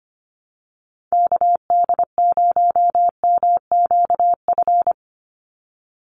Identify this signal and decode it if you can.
Morse code, heard as a keyed tone, about 25 wpm, 700 Hz. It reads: KD0MQF
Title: USA Call Signs 25wpm